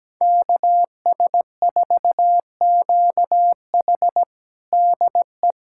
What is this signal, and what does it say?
Morse code, heard as a keyed tone, about 17 wpm, 700 Hz. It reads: KS4QH DE
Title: USA Callsign DE Callsign - 5 Characters - 17wpm